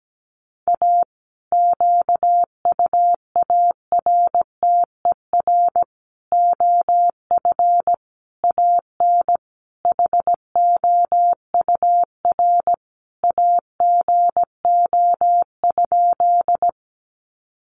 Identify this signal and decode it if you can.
Morse code, heard as a keyed tone, about 17 wpm, 700 Hz. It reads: A QUARTER OF AN HOUR AGO?
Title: Sentences from Top 1300 Words 17wpm